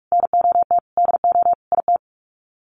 Morse code, heard as a keyed tone, about 44 wpm, 700 Hz. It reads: NOT DO IT
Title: Sets of 5 Words Encore2 45wpm